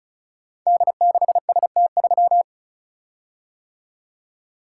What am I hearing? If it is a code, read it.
Morse code, heard as a keyed tone, about 36 wpm, 700 Hz. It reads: D6ST3
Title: Code Groups - 5 Letter:Numbers - 35wpm